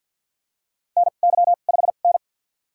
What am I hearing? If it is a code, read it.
Morse code, heard as a keyed tone, about 52 wpm, spaced as for 37 wpm, 700 Hz. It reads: NY5N
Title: USA Call Signs - 4 Characters - 2x Character Spacing - 50wpm